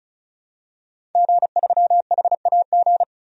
Morse code, heard as a keyed tone, about 34 wpm, 700 Hz. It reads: G3HAG
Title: DXCC-100 Call Signs - 5 Characters - 35wpm